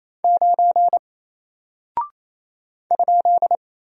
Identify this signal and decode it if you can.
Morse code, heard as a keyed tone, about 27 wpm, 700 Hz. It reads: 9 ?
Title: Single Character (Letter:Number:Punctuation:Prosign) Alt 28wpm